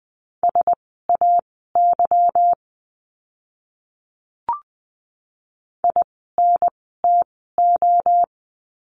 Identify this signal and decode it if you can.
Morse code, heard as a keyed tone, about 20 wpm, spaced as for 15 wpm, 700 Hz. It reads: SAY INTO
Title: 185 Lesson 30f - Letter b - Words 20wpm